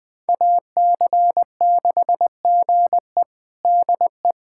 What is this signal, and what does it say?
Morse code, heard as a keyed tone, about 20 wpm, 700 Hz. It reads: AC6GE DE
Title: USA Callsign DE Callsign - 5 Characters - 20wpm